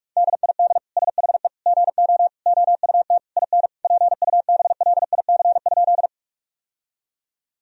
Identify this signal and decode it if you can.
Morse code, heard as a keyed tone, about 46 wpm, 700 Hz. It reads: DID SHE GO OUT IN PUBLIC?
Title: Sentences from Top 1000 Words - Review Part 3 - 45wpm